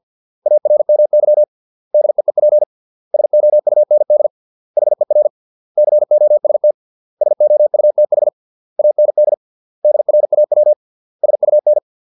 Morse code, heard as a keyed tone, about 50 wpm, 600 Hz. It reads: ARMY DEEP SOUND HER COST SOUTH AND DRAW SUN